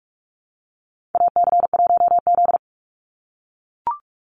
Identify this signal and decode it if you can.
Morse code, heard as a keyed tone, about 44 wpm, 700 Hz. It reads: AC1Z
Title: NA SOTA Activator Callsigns 45wpm